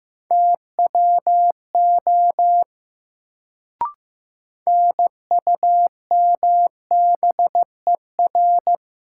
Morse code, heard as a keyed tone, about 15 wpm, 700 Hz. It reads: TWO NUMBER
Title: Sentences from Top 100 Words - New Word Drill - 15wpm